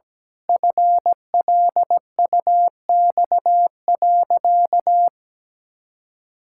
Morse code, heard as a keyed tone, about 17 wpm, 700 Hz. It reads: FLUX.
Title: Pangram Challenge v2 - 17wpm